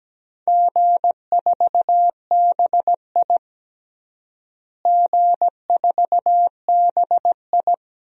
Morse code, heard as a keyed tone, about 17 wpm, 700 Hz. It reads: G4BI G4BI
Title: DXCC-100 Call Signs - 4 Characters - Repeated 2x - 17wpm